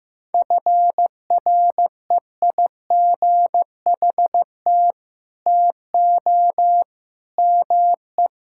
Morse code, heard as a keyed tone, about 15 wpm, 700 Hz. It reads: FREIGHT TO ME
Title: Pangram Challenge v2 - 15wpm